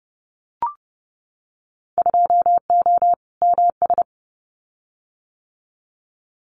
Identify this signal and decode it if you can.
Morse code, heard as a keyed tone, about 30 wpm, 700 Hz. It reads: 2O MS
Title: Winter Field Day Exchanges 30wpm